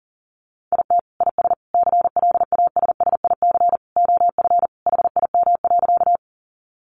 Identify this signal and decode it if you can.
Morse code, heard as a keyed tone, about 40 wpm, 700 Hz. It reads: IT IS CLASSIC OF HIM.
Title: Sentences from Top 1000 Words 40wpm